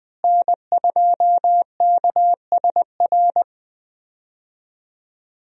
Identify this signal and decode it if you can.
Morse code, heard as a keyed tone, about 21 wpm, 700 Hz. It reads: N2KSR
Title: USA Call Signs - 5 Characters - Repeated 3x - 20wpm